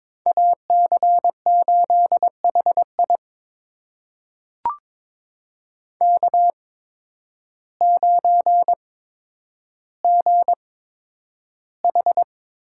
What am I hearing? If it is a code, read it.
Morse code, heard as a keyed tone, about 22 wpm, spaced as for 10 wpm, 700 Hz. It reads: AC8HI K9GH
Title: USA Call Signs - 5 Characters - 8x Character Spacing - 22wpm